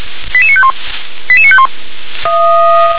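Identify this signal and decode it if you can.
Five-tone selective call (ZVEI), heard as quick runs of five tones, tones of about 70 ms. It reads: 88951 88951
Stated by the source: ZVEI five-tone selective call 88920, received by SDR.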